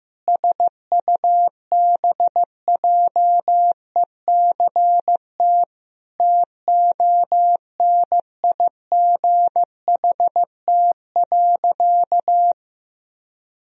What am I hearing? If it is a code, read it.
Morse code, heard as a keyed tone, about 15 wpm, 700 Hz. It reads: SUBJECT TONIGHT.
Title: Sentences from Top 1000 Words - Review Part 4 - 15wpm